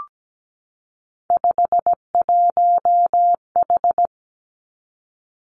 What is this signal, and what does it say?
Morse code, heard as a keyed tone, about 17 wpm, 700 Hz. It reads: 51H
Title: Three Challenging Letter-Number Combinations 17wpm